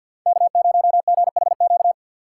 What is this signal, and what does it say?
Morse code, heard as a keyed tone, about 50 wpm, 700 Hz. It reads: K0GHQ
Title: USA Call Signs - 5 Characters - 4x Character Spacing - 50wpm